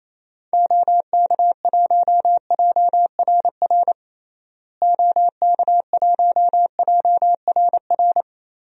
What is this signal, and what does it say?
Morse code, heard as a keyed tone, about 28 wpm, 700 Hz. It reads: OK1JRR OK1JRR
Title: DXCC-100 Call Signs - 6 Characters - Repeated 3x - 28wpm